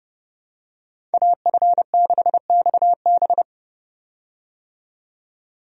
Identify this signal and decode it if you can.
Morse code, heard as a keyed tone, about 30 wpm, 700 Hz. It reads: AF6XB
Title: USA Call Signs - 5 Characters - 30wpm